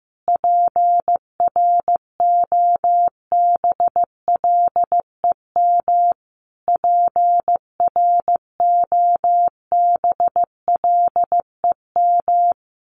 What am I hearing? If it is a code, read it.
Morse code, heard as a keyed tone, about 15 wpm, 700 Hz. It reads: PROBLEM PROBLEM
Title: Top 500 Words Repeated 15wpm